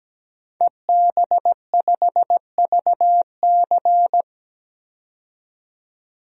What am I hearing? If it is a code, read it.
Morse code, heard as a keyed tone, about 17 wpm, 700 Hz. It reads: EB5VC
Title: DXCC-100 Call Signs - 5 Characters - 17wpm